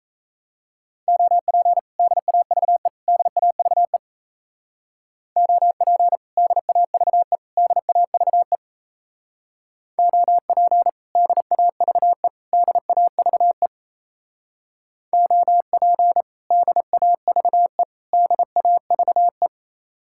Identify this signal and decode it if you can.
Morse code, heard as a keyed tone, about 34 wpm, 700 Hz. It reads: OP DAVE DAVE OP DAVE DAVE OP DAVE DAVE OP DAVE DAVE
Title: Speed-Racing Common QSO Phrases - 28wpm